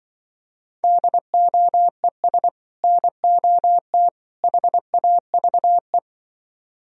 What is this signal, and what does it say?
Morse code, heard as a keyed tone, about 25 wpm, 700 Hz. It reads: DOES NOT HAVE
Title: Speed-Racing - Sets of 3 Words 20wpm